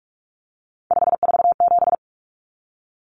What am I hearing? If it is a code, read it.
Morse code, heard as a keyed tone, about 44 wpm, 700 Hz. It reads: 547
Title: Common Cut Numbers - 3-Digit 45wpm